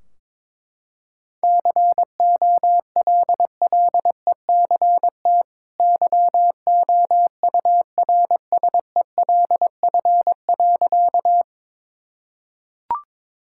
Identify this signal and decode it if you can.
Morse code, heard as a keyed tone, about 22 wpm, 700 Hz. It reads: COLLECT YOURSELF.
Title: Sentences from Top 1600 Words 22wpm